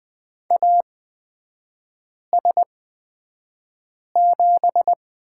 Morse code, heard as a keyed tone, about 20 wpm, spaced as for 5 wpm, 700 Hz. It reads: AS7
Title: 250 Lesson 40e - BK - Three Characters 20wpm